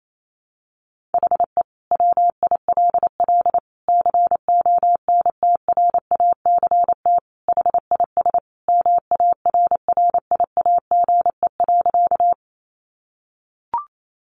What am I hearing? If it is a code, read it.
Morse code, heard as a keyed tone, about 28 wpm, 700 Hz. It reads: HE WILL CONTRACT HIS MARRIAGE.
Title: Sentences from Top 1400 Words 28wpm